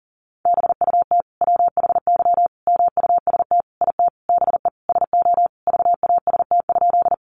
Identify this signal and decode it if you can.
Morse code, heard as a keyed tone, about 40 wpm, 700 Hz. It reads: BUT WHY MUST IT BE SO VAST?
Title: Sentences from Top 1300 Words 40wpm